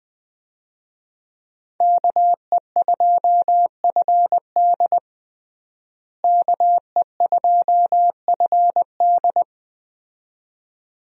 Morse code, heard as a keyed tone, about 20 wpm, 700 Hz. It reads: KE2FD KE2FD
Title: USA Call Signs - 5 Characters - Repeated 2x - 20wpm